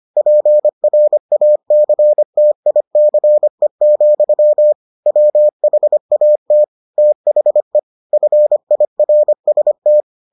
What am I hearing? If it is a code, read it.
Morse code, heard as a keyed tone, about 25 wpm, 600 Hz. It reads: PRACTICE, WHAT THE FIRST